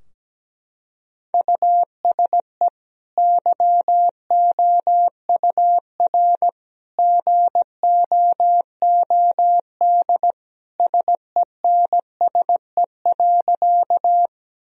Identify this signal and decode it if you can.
Morse code, heard as a keyed tone, about 17 wpm, 700 Hz. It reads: USE YOUR GOOD SENSE.
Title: Sentences from Top 1000 Words - Review Part 4 - 17wpm